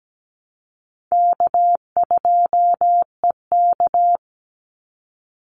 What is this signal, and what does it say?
Morse code, heard as a keyed tone, about 17 wpm, 700 Hz. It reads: K2EK
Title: Easy Call Signs - 17wpm